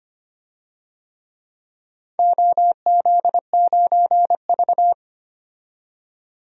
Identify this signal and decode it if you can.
Morse code, heard as a keyed tone, about 25 wpm, 700 Hz. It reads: OZ9V
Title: DXCC-100 Call Signs - 4 Characters - 25wpm